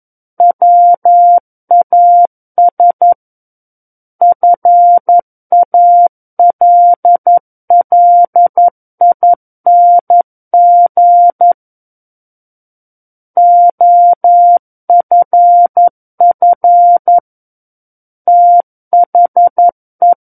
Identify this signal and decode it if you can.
Morse code, heard as a keyed tone, about 11 wpm, 700 Hz. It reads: WAS FALLING OFF THE
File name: 11 WPM